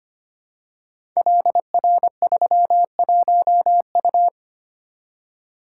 Morse code, heard as a keyed tone, about 25 wpm, 700 Hz. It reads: LR31U
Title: Code Groups - 5 Letter:Numbers - 25wpm